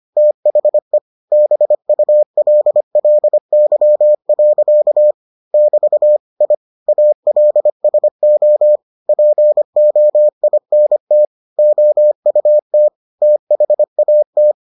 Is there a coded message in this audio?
Yes: Morse code from 0.2 to 14.5 s, heard as a keyed tone, about 25 wpm, 600 Hz. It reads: THE BULLY. <BT> I ALSO POINT OUT THAT